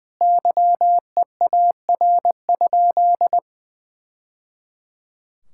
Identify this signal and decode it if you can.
Morse code, heard as a keyed tone, about 20 wpm, 700 Hz. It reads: YEAR?
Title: Sentences from Top 1000 Words - Review Part 6 - 20wpm